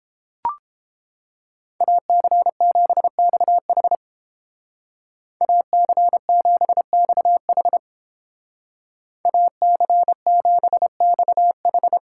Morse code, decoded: AC7XH AC7XH AC7XH

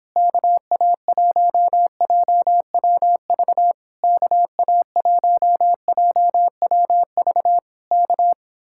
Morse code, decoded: KA1JWV KA1JWV K

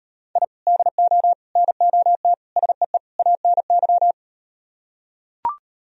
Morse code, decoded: I DO NOT SEE ANY

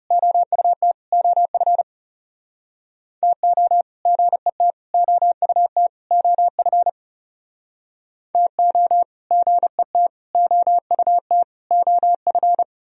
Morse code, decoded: OUT OF TO GET OUT OF TO GET OUT OF

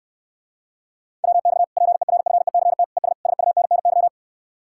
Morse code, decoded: WX PERFECT ES SUNNY

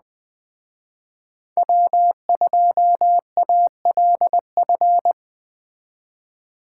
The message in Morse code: W2ALF

